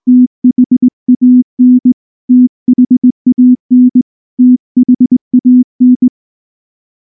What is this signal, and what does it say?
Morse code, heard as a keyed tone, about 20 wpm, 250 Hz. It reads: THAN THAN THAN
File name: morse / morse-training-20WPM-9